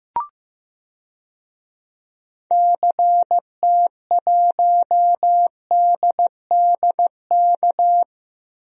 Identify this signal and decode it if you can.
Morse code, heard as a keyed tone, about 15 wpm, 700 Hz. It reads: CT1DDK
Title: DXCC-100 Call Signs - 6 Characters - Repeated 3x - 15wpm